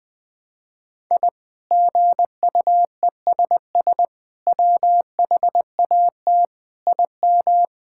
Morse code, heard as a keyed tone, about 21 wpm, 700 Hz. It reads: I GUESS WHAT IM
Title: Sentences with 8-Word Phrases - 20wpm